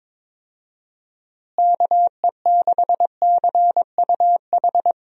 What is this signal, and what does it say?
Morse code, heard as a keyed tone, about 22 wpm, 700 Hz. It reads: KE6CUH
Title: USA Call Signs - 6 Characters - Repeated 3x - 22wpm